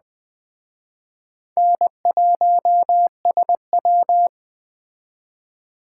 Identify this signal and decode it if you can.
Morse code, heard as a keyed tone, about 21 wpm, 700 Hz. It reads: N1SW
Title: Speed Racing - USA 1x2 Call Signs 17wpm